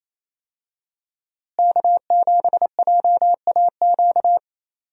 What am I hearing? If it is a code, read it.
Morse code, heard as a keyed tone, about 28 wpm, 700 Hz. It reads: K7JAQ
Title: USA Call Signs - 5 Characters - Repeated 2x - 28wpm